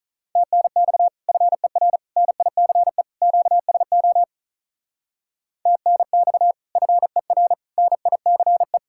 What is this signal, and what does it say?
Morse code, heard as a keyed tone, about 46 wpm, 700 Hz. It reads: TNX FER NICE QSO TNX FER NICE